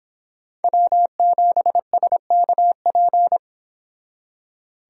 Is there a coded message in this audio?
Yes: Morse code from 0.6 to 3.4 s, heard as a keyed tone, about 30 wpm, 700 Hz. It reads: W7SKP